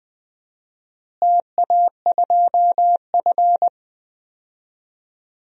Morse code, heard as a keyed tone, about 20 wpm, 700 Hz. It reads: TA2F
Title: DXCC-100 Call Signs - 4 Characters - Repeated 2x - 20wpm